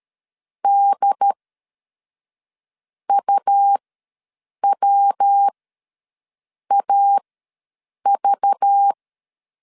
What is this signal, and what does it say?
Morse code, heard as a keyed tone, about 13 wpm, spaced as for 6 wpm, 800 Hz. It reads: DUWAV
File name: LEZIONI / 10 SIGNS / 10D COMA ws 12wpm 3618